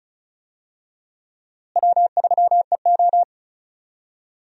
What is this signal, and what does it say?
Morse code, heard as a keyed tone, about 34 wpm, 700 Hz. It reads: W3EO
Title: USA Call Signs - 4 Characters - Repeated 3x - 35wpm